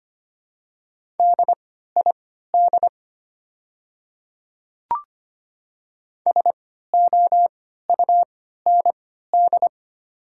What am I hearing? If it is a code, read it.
Morse code, heard as a keyed tone, about 25 wpm, spaced as for 14 wpm, 700 Hz. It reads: DID SOUND
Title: 3x Character Spacing - Top 100 Words - 25wpm